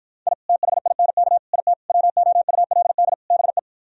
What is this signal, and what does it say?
Morse code, heard as a keyed tone, about 52 wpm, 700 Hz. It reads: I THINK IT WOULD BE